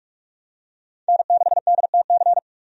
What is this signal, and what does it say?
Morse code, heard as a keyed tone, about 47 wpm, 700 Hz. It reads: N6DTC
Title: USA Call Signs - 5 Characters - 45wpm